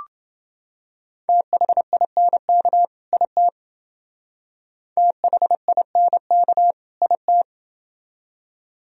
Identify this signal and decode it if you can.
Morse code, heard as a keyed tone, about 28 wpm, 700 Hz. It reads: THINK IT THINK IT